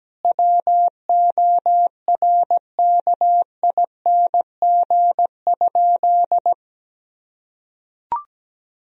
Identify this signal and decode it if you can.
Morse code, heard as a keyed tone, about 17 wpm, 700 Hz. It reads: WORKING?
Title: Sentences from Top 1000 Words - Review Part 4 - 17wpm